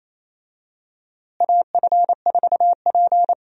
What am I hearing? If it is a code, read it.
Morse code, heard as a keyed tone, about 28 wpm, 700 Hz. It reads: AF4P
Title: USA Call Signs - 4 Characters - Repeated 3x - 28wpm